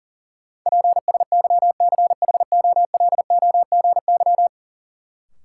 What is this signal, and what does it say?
Morse code, heard as a keyed tone, about 40 wpm, 700 Hz. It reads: PSYCHOLOGY
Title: Sentences from Top 1400 Words - New Word Drill Alt - 40wpm